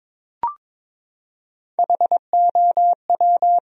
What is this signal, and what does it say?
Morse code, heard as a keyed tone, about 22 wpm, 700 Hz. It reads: HOW
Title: 4x Word-Spacing - Sentences from Top 200 Words - 22wpm